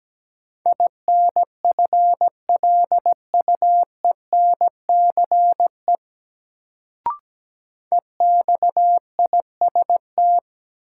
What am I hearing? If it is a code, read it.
Morse code, heard as a keyed tone, about 17 wpm, 700 Hz. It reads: INFLUENCE EXIST
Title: Sentences from Top 1100 Words - New Word Drill Alt - 17wpm